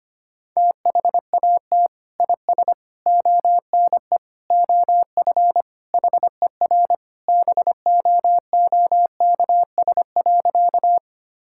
Morse code, heard as a keyed tone, about 25 wpm, 700 Hz. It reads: THAT IS ONE OF HER BOOKS.
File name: practice / 25wpm / Sentences with 2-Word Phrases - 25wpm